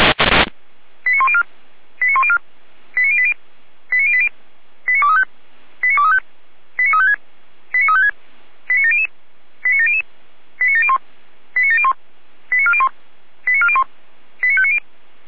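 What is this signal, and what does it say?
Five-tone selective call (ZVEI), heard as quick runs of five tones, tones of about 70 ms. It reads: 89104 89104 89080 89080 89235 89235 89356 89356 89700 89700 89701 89701 89401 89401 89590
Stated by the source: ZVEI five-tone call received by SDR.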